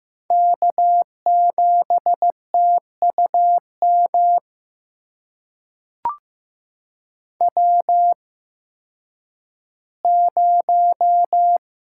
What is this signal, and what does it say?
Morse code, heard as a keyed tone, about 15 wpm, spaced as for 10 wpm, 700 Hz. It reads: K7TUM W0
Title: USA Call Signs - 5 Characters - 8x Character Spacing - 15wpm